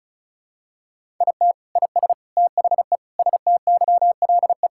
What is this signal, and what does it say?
Morse code, heard as a keyed tone, about 36 wpm, 700 Hz. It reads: IT IS THE STYLE
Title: Sentences from Top 1000 Words - Review Part 5 - 35wpm